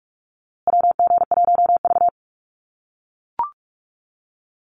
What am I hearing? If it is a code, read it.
Morse code, heard as a keyed tone, about 44 wpm, 700 Hz. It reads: WG1V
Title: USA Call Signs Repeated 3x 45wpm